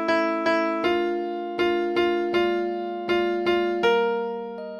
There is no code.